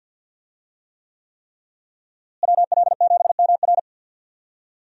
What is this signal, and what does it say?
Morse code, heard as a keyed tone, about 52 wpm, 700 Hz. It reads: WL7MR
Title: USA Call Signs - 5 Characters - Repeated 3x - 50wpm